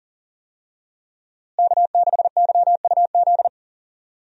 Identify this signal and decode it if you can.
Morse code, heard as a keyed tone, about 39 wpm, 700 Hz. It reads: K6YUZ
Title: USA Call Signs - 5 Characters - Repeated 2x - 40wpm